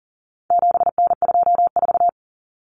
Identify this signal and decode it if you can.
Morse code, heard as a keyed tone, about 40 wpm, 700 Hz. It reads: 7N24
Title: Common Cut Numbers - 4-Digit 40wpm